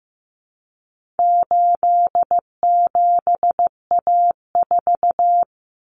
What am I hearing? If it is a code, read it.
Morse code, heard as a keyed tone, about 15 wpm, 700 Hz. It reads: 87A4
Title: Common Cut Numbers - 4-Digit 15wpm